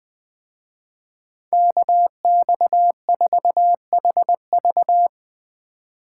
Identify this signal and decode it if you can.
Morse code, heard as a keyed tone, about 23 wpm, 700 Hz. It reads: KX4HV
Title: Speed Racing - USA 2x2 Call Signs 20wpm